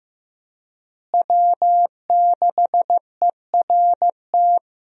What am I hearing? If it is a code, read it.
Morse code, heard as a keyed tone, about 16 wpm, 700 Hz. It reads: W6ERT
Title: USA Call Signs - 5 Characters - Repeated 2x - 15wpm